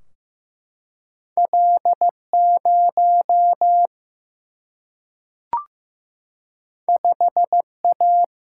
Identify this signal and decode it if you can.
Morse code, heard as a keyed tone, about 15 wpm, 700 Hz. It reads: L0 5A
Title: Single-Letter Single-Number 15wpm